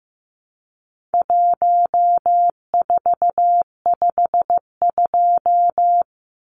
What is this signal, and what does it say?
Morse code, heard as a keyed tone, about 15 wpm, 700 Hz. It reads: 1452
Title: Numbers - Contest Style - 4-Digit 15wpm